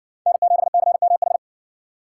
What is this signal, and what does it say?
Morse code, heard as a keyed tone, about 60 wpm, 700 Hz. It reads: N7CMH